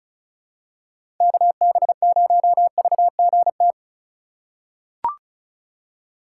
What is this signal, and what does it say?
Morse code, heard as a keyed tone, about 36 wpm, 690 Hz. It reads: KB0VGT
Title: USA Call Signs - 3x Character Spacing - 35wpm